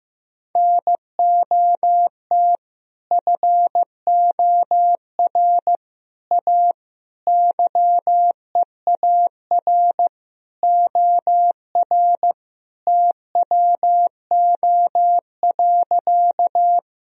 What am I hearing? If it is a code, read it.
Morse code, heard as a keyed tone, about 16 wpm, 700 Hz. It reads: NOT FOR A YEAR OR TWO.
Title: Sentences from Top 1000 Words - Review Part 1 - 15wpm